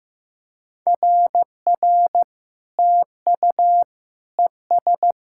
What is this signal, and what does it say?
Morse code, heard as a keyed tone, about 15 wpm, 700 Hz. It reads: RR TU ES
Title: SOTA QSO Practice 15wpm